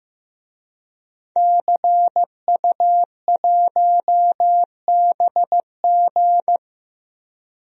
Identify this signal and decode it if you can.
Morse code, heard as a keyed tone, about 15 wpm, 700 Hz. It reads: CU1BG
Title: DXCC-100 Call Signs - 5 Characters - Repeated 2x - 15wpm